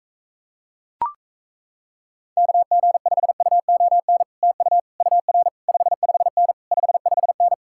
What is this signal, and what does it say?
Morse code, heard as a keyed tone, about 38 wpm, 700 Hz. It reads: KG5UON TU UR 55N 55N